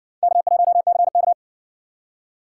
Morse code, heard as a keyed tone, about 60 wpm, 700 Hz. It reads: K1CK